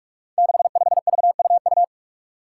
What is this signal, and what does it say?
Morse code, heard as a keyed tone, about 46 wpm, 700 Hz. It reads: 65VUU
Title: Five Challenging Letter-Number Combinations 45wpm